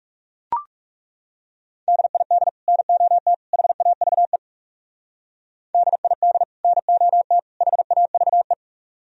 Morse code, heard as a keyed tone, about 34 wpm, 700 Hz. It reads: DID NOT HAVE DID NOT HAVE
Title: Speed-Racing - Sets of 3 Words 30wpm